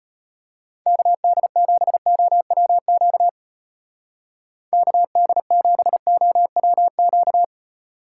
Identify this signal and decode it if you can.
Morse code, heard as a keyed tone, about 36 wpm, 700 Hz. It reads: KD7OWQ KD7OWQ